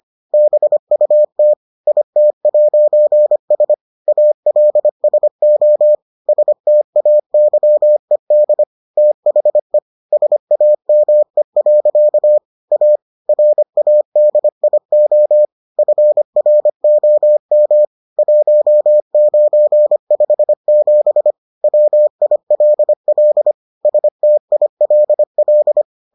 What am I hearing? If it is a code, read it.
Morse code, heard as a keyed tone, about 25 wpm, 600 Hz. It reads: BUT IT'S ALSO STAYED THE SAME. A RADIO FROM 1957 WILL STILL